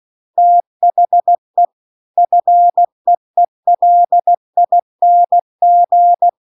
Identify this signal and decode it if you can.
Morse code, heard as a keyed tone, about 16 wpm, 700 Hz. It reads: THE FEELING